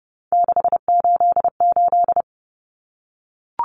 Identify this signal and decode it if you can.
Morse code, heard as a keyed tone, about 30 wpm, 700 Hz. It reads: 688
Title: Numbers - 3-Digit 30wpm